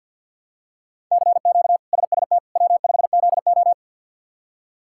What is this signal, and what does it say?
Morse code, heard as a keyed tone, about 52 wpm, 700 Hz. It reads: CQ SST W5ZO
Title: K1USN SST QSO Practice - 50wpm